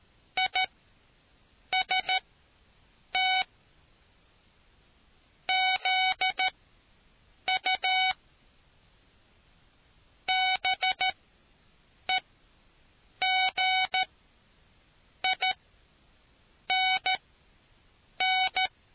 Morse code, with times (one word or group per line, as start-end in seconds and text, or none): 0.4-3.4 IST
5.5-8.1 ZU
10.3-18.7 BEGINN